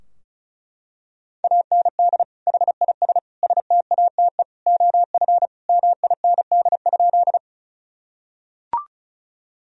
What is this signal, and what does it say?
Morse code, heard as a keyed tone, about 34 wpm, 700 Hz. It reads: AND HIS STATE OF MIND?
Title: Sentences from Top 1000 Words - Review Part 2 - 35wpm